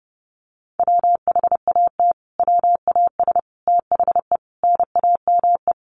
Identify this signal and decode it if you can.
Morse code, heard as a keyed tone, about 30 wpm, 700 Hz. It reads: WHAT WAS THE NAME
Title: Sentences from Top 300 Words 30wpm